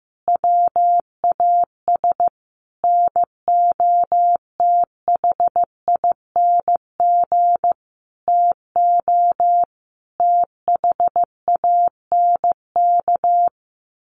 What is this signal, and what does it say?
Morse code, heard as a keyed tone, about 15 wpm, 700 Hz. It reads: WAS NOTHING TO THANK
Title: Sentences from Top 1000 Words 15wpm